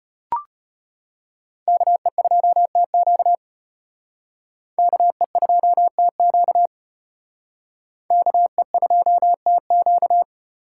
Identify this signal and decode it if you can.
Morse code, heard as a keyed tone, about 34 wpm, 700 Hz. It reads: KE2TQ KE2TQ KE2TQ